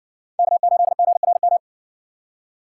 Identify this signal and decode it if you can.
Morse code, heard as a keyed tone, about 60 wpm, 700 Hz. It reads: K8GRM